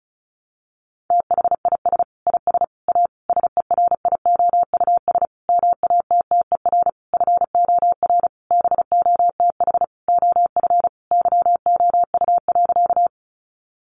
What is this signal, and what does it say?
Morse code, heard as a keyed tone, about 34 wpm, 700 Hz. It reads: THIS IS A SERIOUS MATTER FOR BOTH OF YOU.
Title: Sentences from Top 1100 Words 35wpm